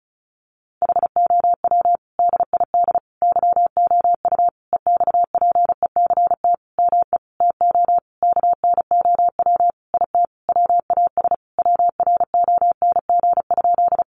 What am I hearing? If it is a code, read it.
Morse code, heard as a keyed tone, about 34 wpm, 700 Hz. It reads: HOW DID YOU EXPECT ME TO KNOW IT WAS WRONG?
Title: Sentences from Top 1100 Words 35wpm